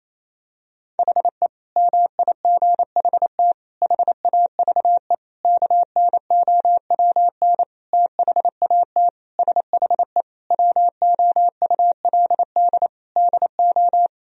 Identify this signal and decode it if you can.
Morse code, heard as a keyed tone, about 28 wpm, 700 Hz. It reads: HE MIGHT HAVE KNOWN THAT SHE WOULD DO
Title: Sentences from Top 1000 Words - Review Part 3 - 28wpm